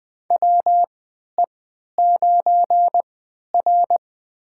Morse code, heard as a keyed tone, about 20 wpm, spaced as for 11 wpm, 700 Hz. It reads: WE9R
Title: USA Call Signs - 3x Character Spacing - 20wpm